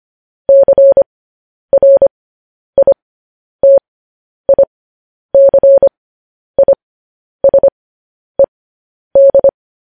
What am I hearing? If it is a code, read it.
Morse code, heard as a keyed tone, about 25 wpm, spaced as for 10 wpm, 550 Hz. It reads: CRITICISED
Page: news-headlines-in-morse-code-at-10-wpm